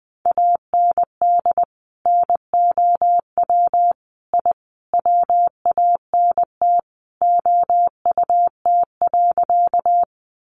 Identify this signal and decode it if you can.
Morse code, heard as a keyed tone, about 20 wpm, 700 Hz. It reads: AND NOW I WANT OUT.
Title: A Sound Of Thunder 20wpm